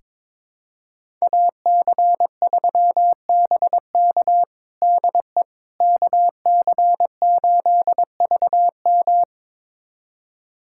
Morse code, heard as a keyed tone, about 22 wpm, 700 Hz. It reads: AC3BK DE KC8VM
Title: USA Callsign DE Callsign - 5 Characters - 22wpm